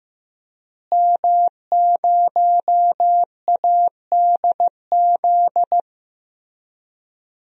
Morse code, heard as a keyed tone, about 15 wpm, 700 Hz. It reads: M0ADZ
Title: DXCC-100 Call Signs - 5 Characters - Repeated 3x - 15wpm